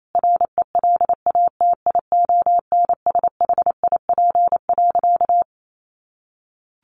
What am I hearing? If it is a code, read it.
Morse code, heard as a keyed tone, about 28 wpm, 700 Hz. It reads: RELATIONSHIP.
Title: Sentences from Top 1100 Words 28wpm